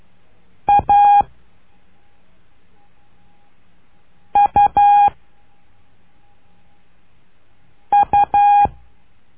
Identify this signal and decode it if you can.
Morse code, heard as a keyed tone, about 12 wpm, 820 Hz. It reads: A U U
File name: cursos / CURSO DE TELEGRAFIA / Telegrafia - Trilha 01